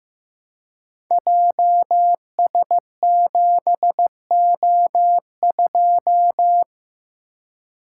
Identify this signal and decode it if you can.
Morse code, heard as a keyed tone, about 15 wpm, 700 Hz. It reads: JS7O2
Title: Code Groups - 5 Letter:Numbers - 15wpm